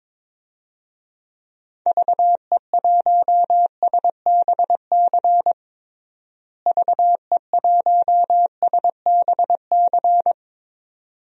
Morse code, heard as a keyed tone, about 22 wpm, 700 Hz. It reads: VE1SBC VE1SBC
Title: DXCC-100 Call Signs - 6 Characters - Repeated 2x - 22wpm